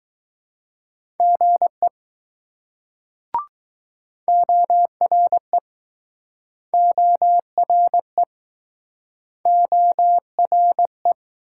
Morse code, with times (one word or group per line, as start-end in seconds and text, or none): 1.2-1.9 GE
4.3-5.6 ORE
6.7-8.2 ORE
9.5-11.1 ORE